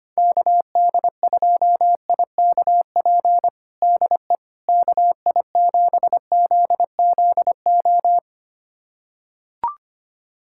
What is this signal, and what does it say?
Morse code, heard as a keyed tone, about 25 wpm, 700 Hz. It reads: KD2IKP DE KI7ZZO
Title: USA Callsign DE Callsign - 6 Characters - 25wpm